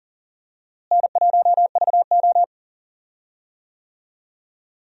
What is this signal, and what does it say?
Morse code, heard as a keyed tone, about 40 wpm, 700 Hz. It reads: N1VO
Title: USA Call Signs - 4 Characters - Repeated 2x - 40wpm